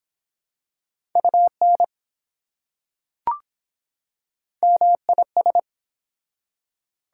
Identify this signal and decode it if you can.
Morse code, heard as a keyed tone, about 23 wpm, 700 Hz. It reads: UN MIS